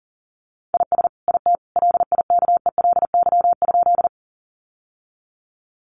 Morse code, heard as a keyed tone, about 40 wpm, 700 Hz. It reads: IS IT LIKELY?
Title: Sentences from Top 600 Words 40wpm